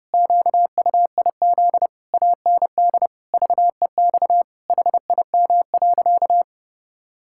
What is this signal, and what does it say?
Morse code, heard as a keyed tone, about 30 wpm, 700 Hz. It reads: QUIZ AND VEX HIM.
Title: Pangram Challenge v2 - 30wpm